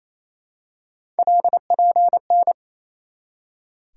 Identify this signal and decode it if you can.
Morse code, heard as a keyed tone, about 28 wpm, 700 Hz. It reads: LPN
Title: Practice QSO - Exchange 3 - More Info - 28wpm